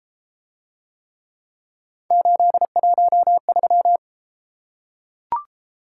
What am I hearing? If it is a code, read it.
Morse code, heard as a keyed tone, about 32 wpm, 700 Hz. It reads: 813